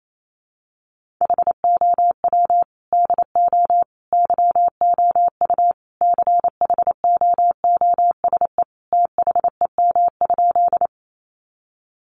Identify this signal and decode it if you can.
Morse code, heard as a keyed tone, about 28 wpm, 700 Hz. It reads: HOW DO YOU CHOOSE THEM?
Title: Sentences from Top 700 Words 28wpm